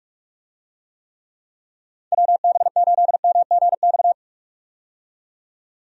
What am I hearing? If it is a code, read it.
Morse code, heard as a keyed tone, about 46 wpm, 700 Hz. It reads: WB8MGX
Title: USA Call Signs - 6 Characters - 45wpm